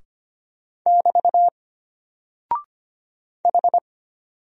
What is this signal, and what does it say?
Morse code, heard as a keyed tone, about 25 wpm, 700 Hz. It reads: <BT> H
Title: Single Character (Letter:Number:Punctuation:Prosign) Alt 25wpm